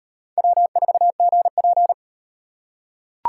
Morse code, heard as a keyed tone, about 38 wpm, 700 Hz. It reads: W4GP